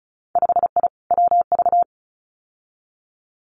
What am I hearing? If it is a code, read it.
Morse code, heard as a keyed tone, about 36 wpm, 700 Hz. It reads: 5I WV
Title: Winter Field Day Exchanges Encore 35wpm